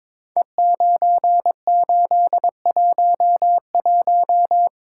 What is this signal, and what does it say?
Morse code, heard as a keyed tone, about 22 wpm, 700 Hz. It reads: E9811
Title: Code Groups - 5 Letter:Numbers - 22wpm